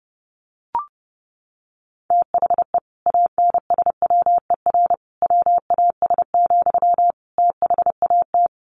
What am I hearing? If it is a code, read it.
Morse code, heard as a keyed tone, about 30 wpm, 700 Hz. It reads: THE ANSWER WAS, THAT
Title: Sentences from Top 1100 Words 30wpm